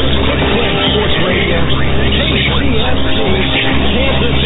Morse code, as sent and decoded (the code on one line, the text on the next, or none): none